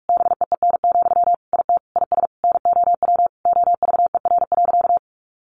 Morse code, heard as a keyed tone, about 44 wpm, 700 Hz. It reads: BEEN, IT IS NOW OVER.
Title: Sentences from Top 1100 Words 45wpm